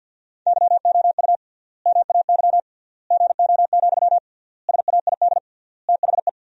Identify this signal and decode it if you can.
Morse code, heard as a keyed tone, about 47 wpm, 700 Hz. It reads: YOU MAY GO, SAID THE
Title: 3x Word-Spacing - Sentences from Top 400 Words - 50wpm